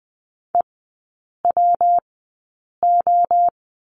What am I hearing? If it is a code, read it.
Morse code, heard as a keyed tone, about 20 wpm, spaced as for 8 wpm, 690 Hz. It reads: EWO
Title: 133 Lesson 22e - Question Mark - Three Characters 20wpm